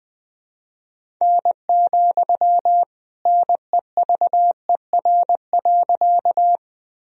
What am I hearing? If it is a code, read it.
Morse code, heard as a keyed tone, about 21 wpm, 700 Hz. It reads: N, NEVER.
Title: Sentences from Top 300 Words and Abbreviations 20wpm